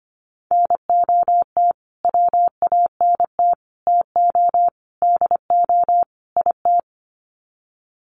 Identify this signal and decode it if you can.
Morse code, heard as a keyed tone, about 25 wpm, 700 Hz. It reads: NOT WANT TO DO IT
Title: Sets of 5 Words Encore 25wpm